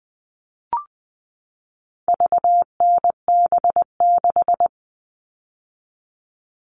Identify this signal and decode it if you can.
Morse code, heard as a keyed tone, about 20 wpm, 700 Hz. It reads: VNB6
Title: All Cut Numbers - 4-Digit 20wpm